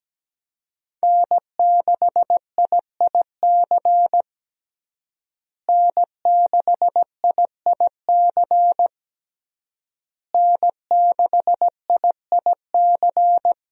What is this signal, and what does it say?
Morse code, heard as a keyed tone, about 17 wpm, 700 Hz. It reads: N6IIC N6IIC N6IIC
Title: USA Call Signs - 5 Characters - Repeated 3x - 17wpm